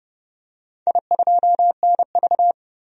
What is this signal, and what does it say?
Morse code, heard as a keyed tone, about 31 wpm, 700 Hz. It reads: I2NV
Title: DXCC-100 Call Signs - 4 Characters - 30wpm